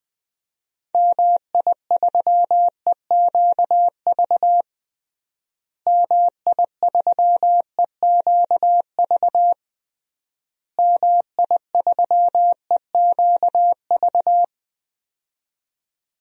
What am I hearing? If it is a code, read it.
Morse code, heard as a keyed tone, about 20 wpm, 700 Hz. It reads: MI3EQV MI3EQV MI3EQV
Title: DXCC-100 Call Signs - 6 Characters - Repeated 3x - 20wpm